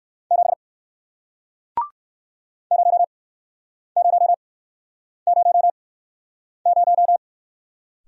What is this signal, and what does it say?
Morse code, heard as a keyed tone, about 61 wpm, spaced as for 9 wpm, 700 Hz. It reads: 7 0000